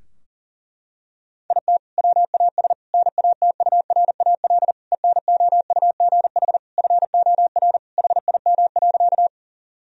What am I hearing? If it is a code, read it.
Morse code, heard as a keyed tone, about 40 wpm, 700 Hz. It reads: IT WAS NATURAL ENOUGH FOR HIM.
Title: Sentences from Top 1000 Words - Review Part 4 - 40wpm